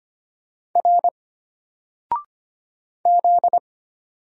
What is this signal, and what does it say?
Morse code, heard as a keyed tone, about 25 wpm, 700 Hz. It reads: R Z
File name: Single-Letter-Number-25wpm